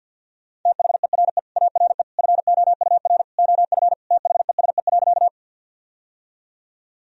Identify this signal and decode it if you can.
Morse code, heard as a keyed tone, about 52 wpm, 700 Hz. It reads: THERE ARE FOUR OF THESE.
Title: Sentences from Top 1000 Words - Review Part 1 - 50wpm